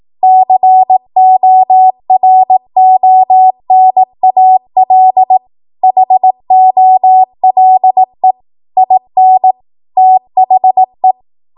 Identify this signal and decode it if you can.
Morse code, heard as a keyed tone, about 18 wpm, 750 Hz. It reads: CORONAL HOLE IN THE
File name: Morse / 18WPM